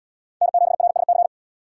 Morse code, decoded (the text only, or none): N7RSC